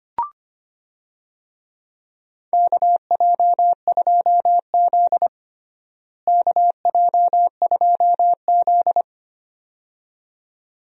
Morse code, heard as a keyed tone, about 25 wpm, 700 Hz. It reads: KJ2Z KJ2Z
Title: USA Call Signs - 4 Characters - Repeated 2x - 25wpm